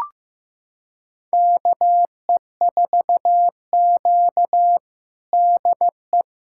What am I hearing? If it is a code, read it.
Morse code, heard as a keyed tone, about 15 wpm, 700 Hz. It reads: KE4Q DE
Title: USA Callsign DE Callsign - 4 Characters - 15wpm